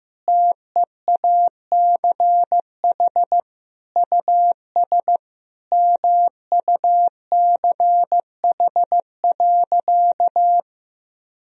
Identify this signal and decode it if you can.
Morse code, heard as a keyed tone, about 15 wpm, 700 Hz. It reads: TEACH US MUCH.
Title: Sentences from Top 1000 Words - Review Part 2 - 15wpm